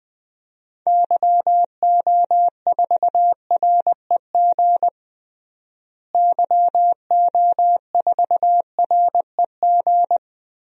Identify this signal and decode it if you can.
Morse code, heard as a keyed tone, about 21 wpm, 700 Hz. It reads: YO4REG YO4REG
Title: DXCC-100 Call Signs - 6 Characters - Repeated 3x - 20wpm